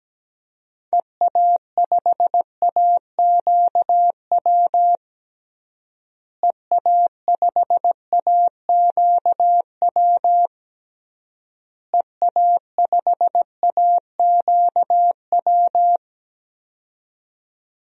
Morse code, heard as a keyed tone, about 17 wpm, 700 Hz. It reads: EA5AQW EA5AQW EA5AQW
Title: DXCC-100 Call Signs - 6 Characters - Repeated 3x - 17wpm